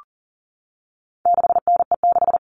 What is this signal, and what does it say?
Morse code, heard as a keyed tone, about 40 wpm, 700 Hz. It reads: 6NE6
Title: All Cut Numbers - 4-Digit 40wpm